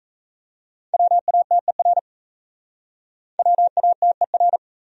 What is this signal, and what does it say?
Morse code, heard as a keyed tone, about 36 wpm, 700 Hz. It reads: WATER WATER